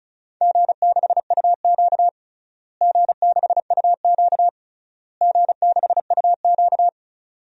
Morse code, heard as a keyed tone, about 34 wpm, 700 Hz. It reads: G6UQ G6UQ G6UQ
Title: DXCC-100 Call Signs - 4 Characters - Repeated 3x - 35wpm